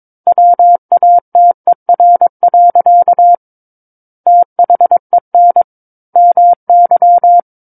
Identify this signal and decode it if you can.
Morse code, heard as a keyed tone, about 22 wpm, 700 Hz. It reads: WATER. THEN MY
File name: Book One / 13-Chapter 13 - 22 WPM